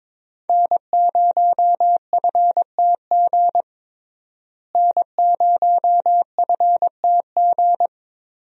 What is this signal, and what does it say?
Morse code, heard as a keyed tone, about 22 wpm, 700 Hz. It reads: N0FTG N0FTG
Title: USA Call Signs - 5 Characters - Repeated 2x - 22wpm